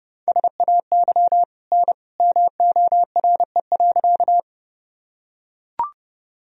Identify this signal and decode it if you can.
Morse code, heard as a keyed tone, about 30 wpm, 700 Hz. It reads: SAY N MORE.
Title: Sentences from Top 200 Words and Abbreviations 30wpm